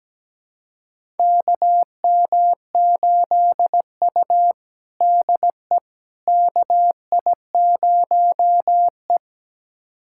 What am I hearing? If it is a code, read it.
Morse code, heard as a keyed tone, about 17 wpm, 700 Hz. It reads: KM8U DE KI0E
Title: USA Callsign DE Callsign - 4 Characters - 17wpm